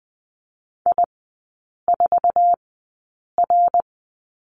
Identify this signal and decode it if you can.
Morse code, heard as a keyed tone, about 20 wpm, spaced as for 8 wpm, 700 Hz. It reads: I4R
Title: 121 Lesson 20e - Number 3 - Three Characters 20wpm